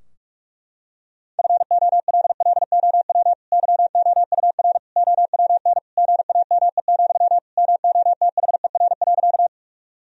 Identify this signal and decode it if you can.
Morse code, heard as a keyed tone, about 42 wpm, 700 Hz. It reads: FOLLOW YOUR OWN GAME, MOTHER.
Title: Sentences from Top 1000 Words - Review Part 3 - 45wpm